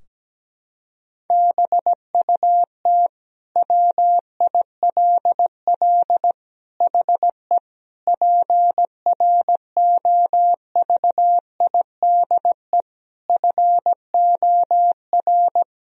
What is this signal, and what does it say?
Morse code, heard as a keyed tone, about 17 wpm, 700 Hz. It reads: BUT WILL HE PROVIDE FOR
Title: Sentences from Top 1000 Words - Review Part 3 - 17wpm